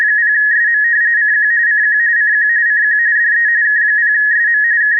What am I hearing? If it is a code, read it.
RTTY, about 45.45 baud. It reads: 0SM CQ CQ CQ CQ DE K0SM K
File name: k3 afsk / buzz with filter